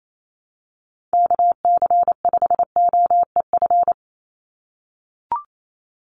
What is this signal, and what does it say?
Morse code, heard as a keyed tone, about 28 wpm, 700 Hz. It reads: KC5OEF
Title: USA Call Signs 28wpm